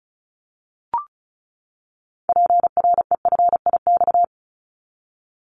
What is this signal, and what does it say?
Morse code, heard as a keyed tone, about 36 wpm, 700 Hz. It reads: PREFIX
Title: Callsign Prefixes for Top 25 DXCC - 35wpm